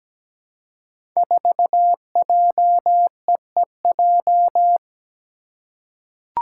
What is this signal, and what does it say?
Morse code, heard as a keyed tone, about 17 wpm, 700 Hz. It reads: 4JEEJ
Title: Five Challenging Letter-Number Combinations 17wpm